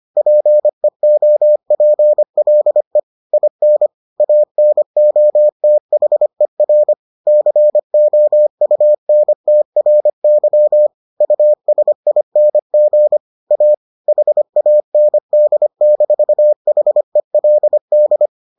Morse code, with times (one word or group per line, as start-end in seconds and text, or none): 0.2-3.0 PEOPLE
3.3-3.9 IN
4.2-6.9 ANOTHER
7.3-10.9 COUNTRY
11.2-13.2 USING
13.5-13.7 A
14.1-18.3 HAND-HELD